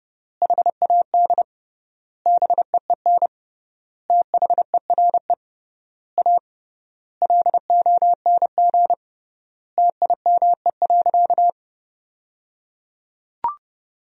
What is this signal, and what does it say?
Morse code, heard as a keyed tone, about 30 wpm, 700 Hz. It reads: HAD BEEN THERE A LONG TIME.
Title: 3x Word-Spacing - Sentences from Top 400 Words - 30wpm